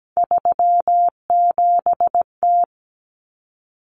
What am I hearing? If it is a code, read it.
Morse code, heard as a keyed tone, about 17 wpm, 700 Hz. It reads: 37T
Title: Common Cut Numbers - 3-Digit 17wpm